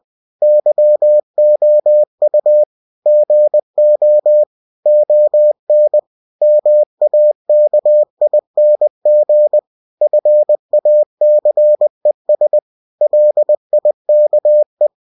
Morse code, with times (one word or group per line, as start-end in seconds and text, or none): 0.4-2.6 YOU
3.1-4.4 GO
4.9-6.0 ON
6.4-9.6 MAKING
10.0-12.6 FACES
13.0-14.9 LIKE